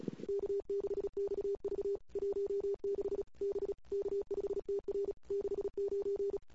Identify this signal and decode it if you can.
Morse code, heard as a keyed tone, about 35 wpm, 400 Hz. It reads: K6XV 16 B K5TR 69
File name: CW / 2008-ss-cw-k5tr-op-n5rz -21:00